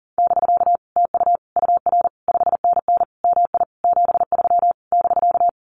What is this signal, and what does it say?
Morse code, heard as a keyed tone, about 40 wpm, 700 Hz. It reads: <BK> TU UR 5NN MI 73 <BK>
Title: Common QSO Phrases Encore 40wpm